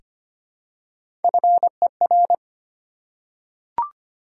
Morse code, heard as a keyed tone, about 25 wpm, 700 Hz. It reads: FER
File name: Top 20 QSO Elements - 3x Character Spacing - 25wpm